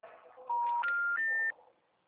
Special information tone.
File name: out- -2018- - - .17305